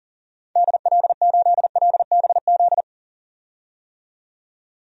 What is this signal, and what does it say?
Morse code, heard as a keyed tone, about 40 wpm, 700 Hz. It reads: DL8LBZ
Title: DXCC-100 Call Signs - 6 Characters - 40wpm